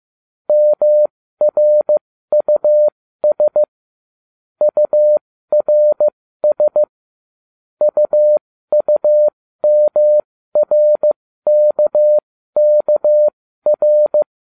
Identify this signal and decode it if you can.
Morse code, heard as a keyed tone, about 15 wpm, spaced as for 13 wpm, 600 Hz. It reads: MRUS URS UUMRKKR